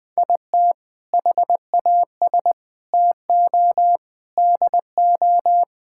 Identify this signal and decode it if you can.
Morse code, heard as a keyed tone, about 20 wpm, 700 Hz. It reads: IT HAS TO DO
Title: Sentences with 5-Word Phrases - 20wpm